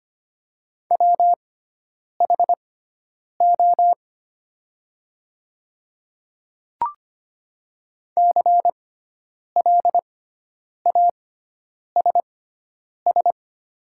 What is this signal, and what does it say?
Morse code, heard as a keyed tone, about 25 wpm, spaced as for 9 wpm, 700 Hz. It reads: WHO CLASS
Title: 6x Character Spacing - Top 500 Words - 25wpm